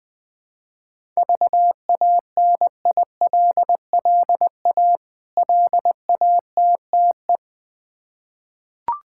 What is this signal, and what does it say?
Morse code, heard as a keyed tone, about 20 wpm, 700 Hz. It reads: VANILLA LATTE
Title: Difficult Words with Hints - 20wpm